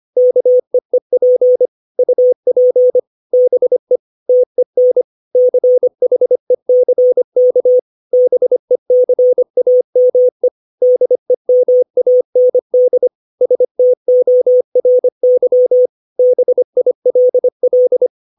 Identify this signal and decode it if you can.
Morse code, heard as a keyed tone, about 25 wpm, 500 Hz. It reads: KEEP UP BE TEN CHECK BECAME DEMAND STORY BILL
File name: 632 words / 632 Words 25WPM